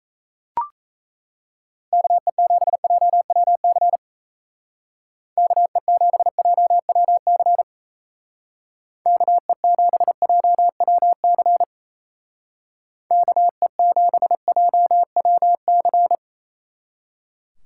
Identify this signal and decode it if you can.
Morse code, heard as a keyed tone, about 34 wpm, 700 Hz. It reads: KE7JWC KE7JWC KE7JWC KE7JWC
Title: Speed Racing - USA 2x3 Call Signs 28wpm